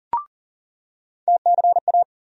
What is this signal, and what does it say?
Morse code, heard as a keyed tone, about 40 wpm, 700 Hz. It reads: TCA
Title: USA Call Sign Postfixes - 3 Letters 40wpm